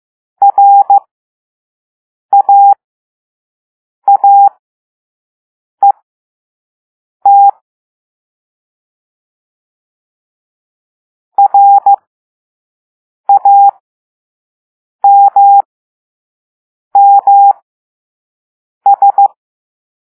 Morse code, heard as a keyed tone, about 15 wpm, spaced as for 5 wpm, 800 Hz. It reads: RAAET RAMMS